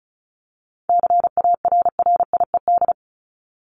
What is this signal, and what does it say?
Morse code, heard as a keyed tone, about 34 wpm, 700 Hz. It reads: CARRIED
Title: ED Suffix - Top 100 Words with Base Words 35wpm